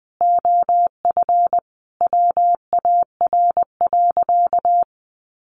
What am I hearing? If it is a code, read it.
Morse code, heard as a keyed tone, about 20 wpm, 700 Hz. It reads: OF WAR.
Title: Sentences from Top 1500 Words 20wpm